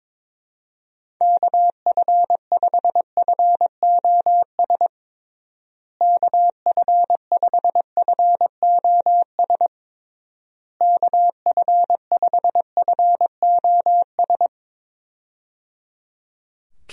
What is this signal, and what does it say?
Morse code, heard as a keyed tone, about 21 wpm, 700 Hz. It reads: KF5FOS KF5FOS KF5FOS
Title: USA Call Signs - 6 Characters - Repeated 3x - 22wpm